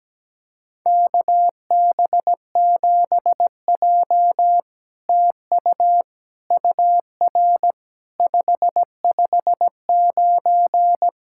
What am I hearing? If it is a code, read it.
Morse code, heard as a keyed tone, about 17 wpm, 700 Hz. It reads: KB7J TU UR 559
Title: SOTA QSO Practice 17wpm